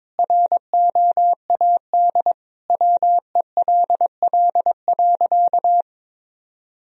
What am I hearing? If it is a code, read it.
Morse code, heard as a keyed tone, about 22 wpm, 700 Hz. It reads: ROAD WELL.
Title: Sentences from Top 1000 Words - Review Part 2 - 22wpm